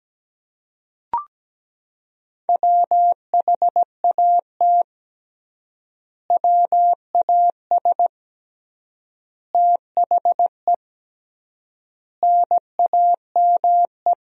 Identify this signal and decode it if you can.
Morse code, heard as a keyed tone, about 17 wpm, 700 Hz. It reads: WHAT WAS THE NAME
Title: 3x Word-Spacing - Sentences from Top 300 Words - 17wpm